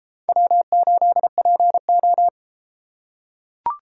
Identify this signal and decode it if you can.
Morse code, heard as a keyed tone, about 33 wpm, 700 Hz. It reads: W8PO